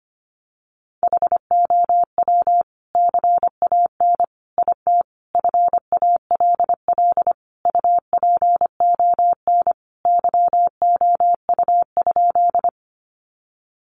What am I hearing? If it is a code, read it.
Morse code, heard as a keyed tone, about 25 wpm, 700 Hz. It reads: HOW CAN IT FALL UPON YOU?
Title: Sentences from Top 1100 Words 25wpm